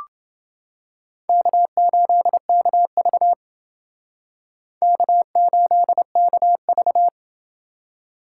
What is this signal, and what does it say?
Morse code, heard as a keyed tone, about 28 wpm, 700 Hz. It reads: K8KV K8KV